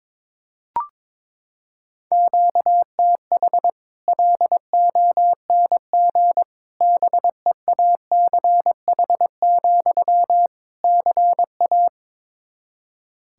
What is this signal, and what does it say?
Morse code, heard as a keyed tone, about 22 wpm, 700 Hz. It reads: QTH LONG BEACH, CA
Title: Practice QSO - Exchange 1 - Beginning - 22wpm